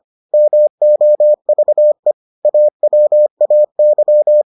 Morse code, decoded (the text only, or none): MOVE AWAY